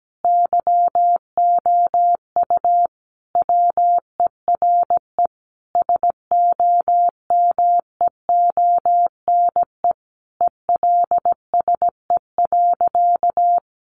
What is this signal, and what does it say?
Morse code, heard as a keyed tone, about 17 wpm, 700 Hz. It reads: YOU WERE SOMEONE ELSE.
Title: Sentences from Top 800 Words 17wpm